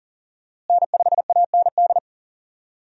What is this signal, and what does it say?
Morse code, heard as a keyed tone, about 38 wpm, 700 Hz. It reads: N5AND
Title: USA Call Signs - 5 Characters - Repeated 2x - 40wpm